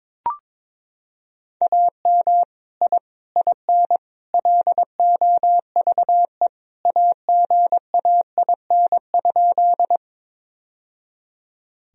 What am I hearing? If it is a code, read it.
Morse code, heard as a keyed tone, about 22 wpm, 700 Hz. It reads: AM I IN LOVE AGAIN?
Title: Sentences from Top 1000 Words - Review Part 2 - 22wpm